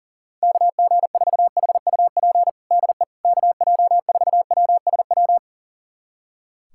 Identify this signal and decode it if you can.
Morse code, heard as a keyed tone, about 40 wpm, 700 Hz. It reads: KG4HUP DE KJ4WSW
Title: USA Callsign DE Callsign - 6 Characters - 40wpm